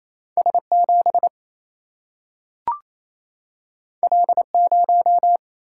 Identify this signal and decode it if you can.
Morse code, heard as a keyed tone, about 28 wpm, 700 Hz. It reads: S 7 L 0
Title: Single-Letter Single-Number 28wpm